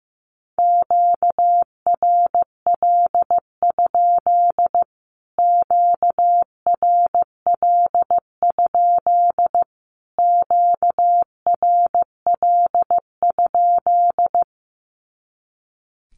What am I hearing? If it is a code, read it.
Morse code, heard as a keyed tone, about 15 wpm, 700 Hz. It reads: QRL? QRL? QRL?
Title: Q-Codes Repeated 3x 15wpm